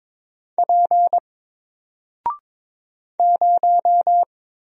Morse code, heard as a keyed tone, about 22 wpm, 700 Hz. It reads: P 0
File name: Single-Letter-Number-22wpm